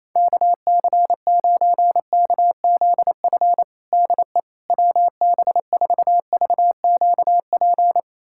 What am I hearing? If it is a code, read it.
Morse code, heard as a keyed tone, about 28 wpm, 700 Hz. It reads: KC9KZF DE WB4VQP
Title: USA Callsign DE Callsign - 6 Characters - 28wpm